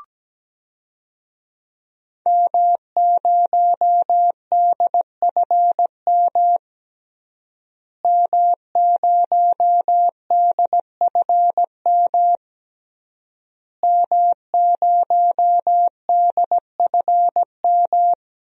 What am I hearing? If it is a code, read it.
Morse code, heard as a keyed tone, about 17 wpm, 700 Hz. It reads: M0DFM M0DFM M0DFM
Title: DXCC-100 Call Signs - 5 Characters - Repeated 3x - 17wpm